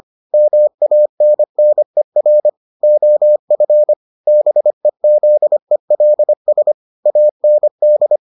Morse code, heard as a keyed tone, about 25 wpm, 600 Hz. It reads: MANNER OF BEZELS AND